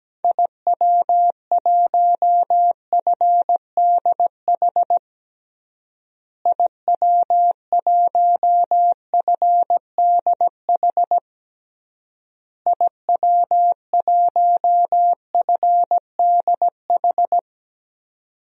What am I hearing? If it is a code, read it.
Morse code, heard as a keyed tone, about 17 wpm, 700 Hz. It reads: IW1FDH IW1FDH IW1FDH
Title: DXCC-100 Call Signs - 6 Characters - Repeated 3x - 17wpm